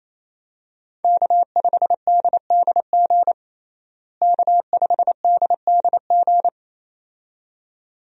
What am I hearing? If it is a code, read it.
Morse code, heard as a keyed tone, about 28 wpm, 700 Hz. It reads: K5DDG K5DDG
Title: USA Call Signs - 5 Characters - Repeated 2x - 28wpm